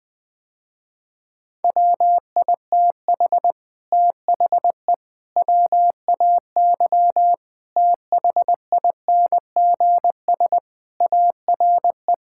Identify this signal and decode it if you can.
Morse code, heard as a keyed tone, about 20 wpm, 700 Hz. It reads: WITH THE WAY THINGS ARE
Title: Sentences with 8-Word Phrases - Spoken Beginning - 20wpm